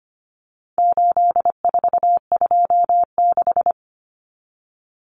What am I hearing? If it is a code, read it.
Morse code, heard as a keyed tone, about 25 wpm, 700 Hz. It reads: 8426
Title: Numbers - 4-Digit 25wpm